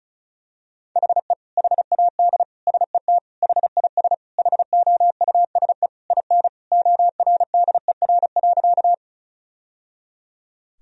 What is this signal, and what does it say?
Morse code, heard as a keyed tone, about 34 wpm, 700 Hz. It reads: HE HAD SET HIS HOUSE IN ORDER.
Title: Sentences from Top 1000 Words - Review Part 2 - 35wpm